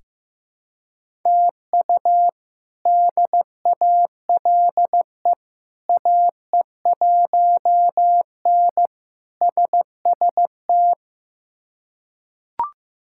Morse code, heard as a keyed tone, about 15 wpm, 700 Hz. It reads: TU DALE AE1N SST
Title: K1USN SST QSO Practice - 15wpm